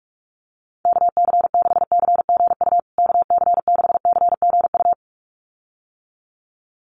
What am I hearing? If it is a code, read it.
Morse code, heard as a keyed tone, about 44 wpm, 690 Hz. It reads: KC6CGU KC6CGU
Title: USA Call Signs Repeated 2x 45wpm